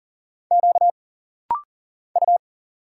Morse code, heard as a keyed tone, about 40 wpm, 700 Hz. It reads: Q U
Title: Single Letter-Number Alt - Rapid-Fire 40wpm